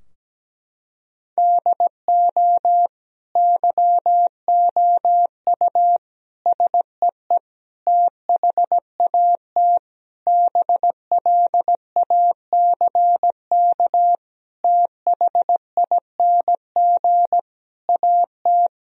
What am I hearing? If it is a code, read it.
Morse code, heard as a keyed tone, about 17 wpm, 700 Hz. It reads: DO YOU SEE THAT BLACK THING AT
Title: Sentences from Top 1000 Words - Review Part 2 - 17wpm